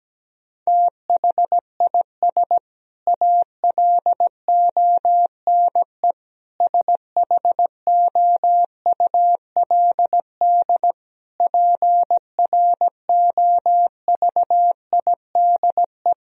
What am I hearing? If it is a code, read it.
Morse code, heard as a keyed tone, about 17 wpm, 700 Hz. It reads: THIS ALONE SHOULD PROVIDE
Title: Sentences from Top 1000 Words - Review Part 4 - 17wpm